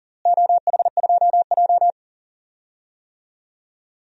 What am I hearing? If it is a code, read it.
Morse code, heard as a keyed tone, about 40 wpm, 700 Hz. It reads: OH2J
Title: DXCC-100 Call Signs - 4 Characters - Repeated 3x - 40wpm